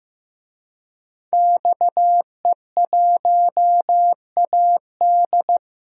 Morse code, heard as a keyed tone, about 15 wpm, 700 Hz. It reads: XE1AD
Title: DXCC-100 Call Signs - 5 Characters - Repeated 2x - 15wpm